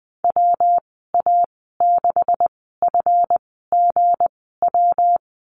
Morse code, heard as a keyed tone, about 20 wpm, spaced as for 15 wpm, 700 Hz. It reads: WA6FGW
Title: 167 Lesson 27g - Number 7 - Call Signs 20wpm